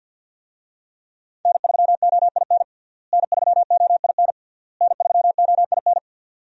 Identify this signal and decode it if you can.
Morse code, heard as a keyed tone, about 52 wpm, 700 Hz. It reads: N3OIN N3OIN N3OIN
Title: USA Call Signs - 5 Characters - Repeated 3x - 50wpm